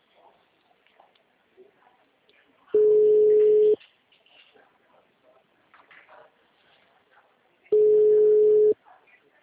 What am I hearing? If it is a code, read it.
Ringback tone (Europe, 425 Hz), with speech.